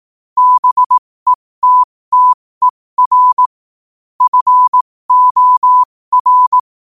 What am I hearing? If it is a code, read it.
Morse code, heard as a keyed tone, about 18 wpm, spaced as for 15 wpm, 1000 Hz. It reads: BETTER FOR
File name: qotdfiles / 15wpm